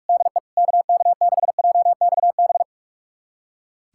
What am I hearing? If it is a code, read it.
Morse code, heard as a keyed tone, about 41 wpm, 690 Hz. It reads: DE KK6JXB
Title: USA Callsign DE Callsign 45wpm